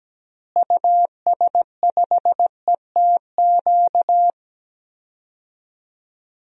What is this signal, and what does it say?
Morse code, heard as a keyed tone, about 17 wpm, 700 Hz. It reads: US5ETQ
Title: DXCC-100 Call Signs - 6 Characters - Repeated 3x - 17wpm